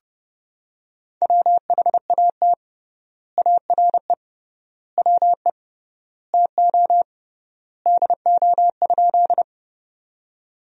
Morse code, heard as a keyed tone, about 30 wpm, 700 Hz. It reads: WHAT ARE WE TO DO?
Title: 3x Word-Spacing - Sentences from Top 100 Words - 30wpm